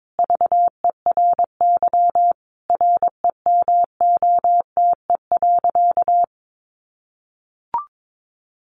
Morse code, heard as a keyed tone, about 22 wpm, 700 Hz. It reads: VERY REMOTE.
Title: Sentences from Top 1300 Words 22wpm